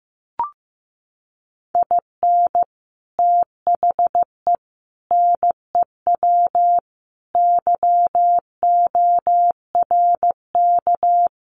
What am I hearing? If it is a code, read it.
Morse code, heard as a keyed tone, about 15 wpm, 700 Hz. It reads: IN THE NEW YORK
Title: Sets of 5 Words 15wpm